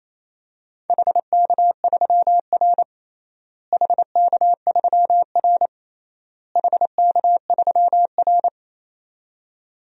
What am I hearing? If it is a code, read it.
Morse code, heard as a keyed tone, about 27 wpm, 700 Hz. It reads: HK3R HK3R HK3R
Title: DXCC-100 Call Signs - 4 Characters - Repeated 3x - 28wpm